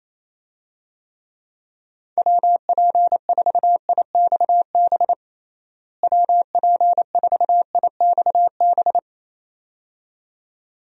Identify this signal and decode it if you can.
Morse code, heard as a keyed tone, about 28 wpm, 700 Hz. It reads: WP4IXB WP4IXB
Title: USA Call Signs - 6 Characters - Repeated 2x - 28wpm